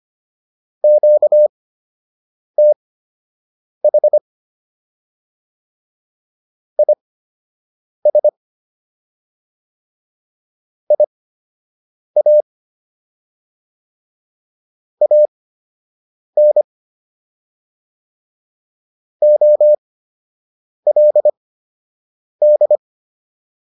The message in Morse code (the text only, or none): QTH IS IA AN OLD